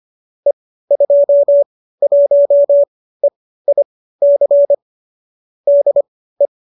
Morse code, heard as a keyed tone, about 25 wpm, spaced as for 15 wpm, 570 Hz. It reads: E21EIC DE
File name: qso208 15